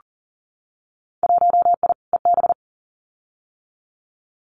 Morse code, heard as a keyed tone, about 40 wpm, 700 Hz. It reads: 1I EB
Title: Winter Field Day Exchanges Encore 40wpm